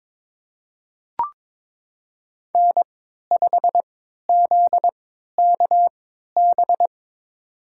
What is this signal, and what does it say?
Morse code, heard as a keyed tone, about 22 wpm, spaced as for 13 wpm, 700 Hz. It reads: N5ZKB
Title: USA Call Signs - 5 Characters - 3x Character Spacing - 22wpm